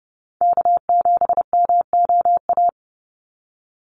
Morse code, heard as a keyed tone, about 30 wpm, 700 Hz. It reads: K7MOA
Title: Call Signs Repeated 2x - 30wpm